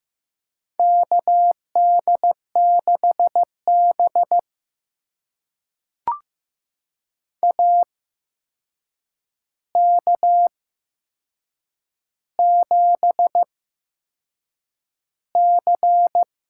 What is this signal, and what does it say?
Morse code, heard as a keyed tone, about 15 wpm, spaced as for 6 wpm, 700 Hz. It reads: KD6B AK7C
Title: USA Call Signs - 4 Characters - 8x Character Spacing - 15wpm